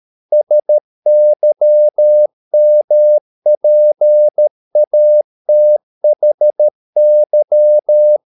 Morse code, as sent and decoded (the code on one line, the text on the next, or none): ... -.-- -- .--. .- - .... -.--
SYMPATHY